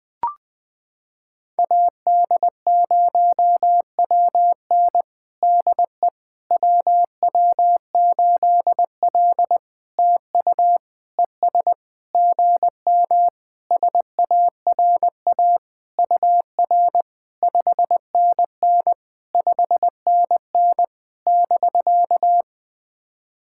Morse code, heard as a keyed tone, about 20 wpm, 700 Hz. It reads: AD0WN DE WW8L TU ES GM SARA UR 5NN 5NN <BK>
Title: POTA QSO Practice - 20wpm